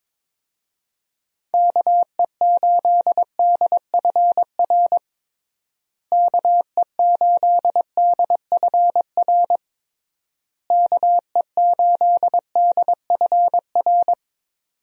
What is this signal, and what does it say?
Morse code, heard as a keyed tone, about 22 wpm, 700 Hz. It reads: KE8DFR KE8DFR KE8DFR
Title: USA Call Signs - 6 Characters - Repeated 3x - 22wpm